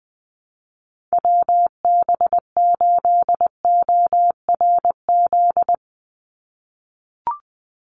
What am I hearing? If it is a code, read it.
Morse code, heard as a keyed tone, about 20 wpm, 700 Hz. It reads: WB8ORZ
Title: USA Call Signs 20wpm